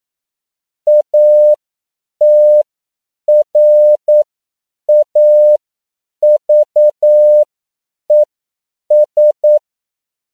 Morse code, heard as a keyed tone, about 9 wpm, spaced as for 7 wpm, 600 Hz. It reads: ATRAVES
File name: CW / 10PPM